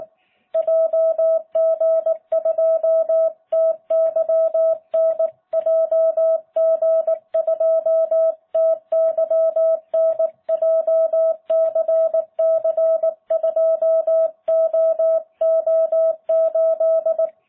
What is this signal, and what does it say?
Morse code, heard as a keyed tone, about 19 wpm, 650 Hz. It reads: JG2TYN JG2TYN JCC2OO8